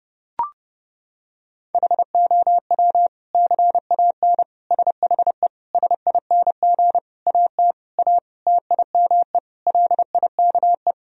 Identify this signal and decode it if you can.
Morse code, heard as a keyed tone, about 30 wpm, 700 Hz. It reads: HOW CAN SHE SING AT A TIME LIKE
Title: Sentences from Top 1000 Words - Review Part 2 - 30wpm